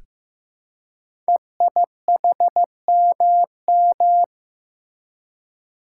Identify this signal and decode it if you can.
Morse code, heard as a keyed tone, about 15 wpm, 700 Hz. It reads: EIHMM
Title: Code Groups - 5 Letters - 15wpm